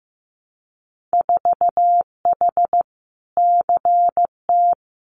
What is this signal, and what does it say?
Morse code, heard as a keyed tone, about 15 wpm, 700 Hz. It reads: 4H CT
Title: Winter Field Day Exchanges Encore 15wpm